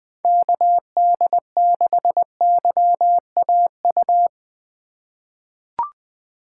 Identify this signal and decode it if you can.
Morse code, heard as a keyed tone, about 21 wpm, 700 Hz. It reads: KD6YAU
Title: USA Call Signs - 6 Characters - 3x Character Spacing - 20wpm